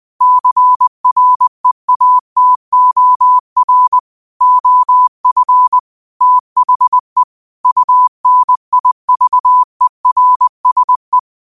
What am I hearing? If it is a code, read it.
Morse code, heard as a keyed tone, about 20 wpm, 1000 Hz. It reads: CREATOR OF THE UNIVERSE